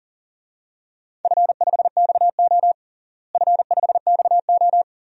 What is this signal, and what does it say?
Morse code, heard as a keyed tone, about 39 wpm, 690 Hz. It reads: F5XO F5XO
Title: DXCC-100 Call Signs - 4 Characters - Repeated 2x - 40wpm